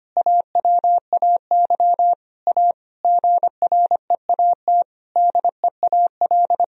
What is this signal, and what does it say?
Morse code, heard as a keyed tone, about 25 wpm, 700 Hz. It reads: AWAY A GREAT DEAL
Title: Sentences from Top 1000 Words - Review Part 1 - 25wpm